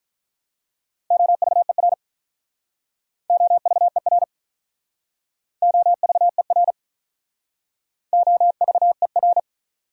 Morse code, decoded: OVER OVER OVER OVER

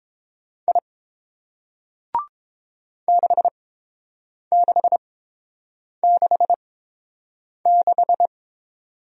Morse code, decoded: I 6666

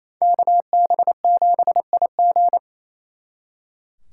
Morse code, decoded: KB7IG

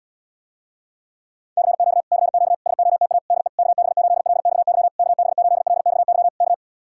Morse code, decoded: CQ CQ SOTA DE KB8LXY KB8LXY K

